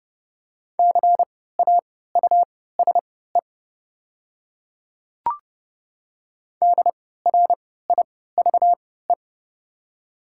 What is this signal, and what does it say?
Morse code, heard as a keyed tone, about 30 wpm, spaced as for 17 wpm, 690 Hz. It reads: CAUSE DRIVE
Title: 3x Character Spacing - Top 500 Words - 30wpm